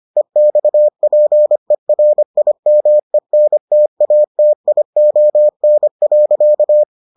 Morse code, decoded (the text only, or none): EXPERIMENTATION.